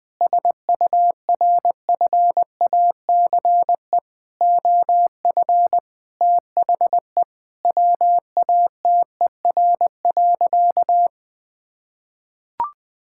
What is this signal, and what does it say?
Morse code, heard as a keyed tone, about 20 wpm, 700 Hz. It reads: SURFACE OF THE WATER.
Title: Sentences with 4-Word Phrases - 20wpm